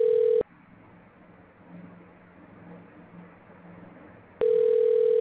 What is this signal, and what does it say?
Ringback tone (North America), 440 + 480 Hz.